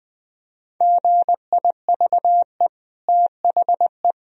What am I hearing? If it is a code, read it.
Morse code, heard as a keyed tone, about 21 wpm, 700 Hz. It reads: GIVE THE
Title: Sentences from Top 1000 Words - Review Part 2 - 20wpm